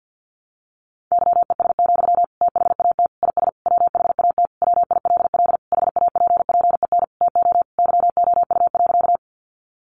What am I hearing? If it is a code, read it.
Morse code, heard as a keyed tone, about 50 wpm, 700 Hz. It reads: YES, THAT IS WHAT WILL HAPPEN TO YOU.
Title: Sentences from Top 500 Words 50wpm